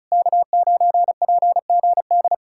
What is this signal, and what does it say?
Morse code, heard as a keyed tone, about 40 wpm, 700 Hz. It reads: K9PGD